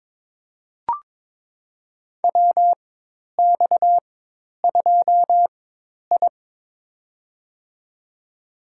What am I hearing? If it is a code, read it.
Morse code, heard as a keyed tone, about 22 wpm, spaced as for 10 wpm, 700 Hz. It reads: WX2I
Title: USA Call Signs - 4 Characters - 4x Character Spacing - 22wpm